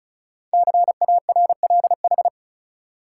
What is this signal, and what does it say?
Morse code, heard as a keyed tone, about 36 wpm, 700 Hz. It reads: CARLH
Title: Code Groups - 5 Letters - 35wpm